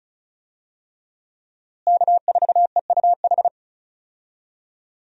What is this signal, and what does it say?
Morse code, heard as a keyed tone, about 36 wpm, 700 Hz. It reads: K4EUH
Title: USA Call Signs - 5 Characters - Repeated 3x - 35wpm